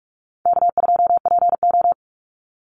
Morse code, heard as a keyed tone, about 46 wpm, 700 Hz. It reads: K2PO
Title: Call Signs - 45wpm